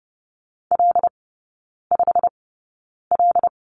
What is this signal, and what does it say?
Morse code, heard as a keyed tone, about 30 wpm, spaced as for 9 wpm, 700 Hz. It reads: L5L
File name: Lesson 15e - Number 5 - Three Characters 30wpm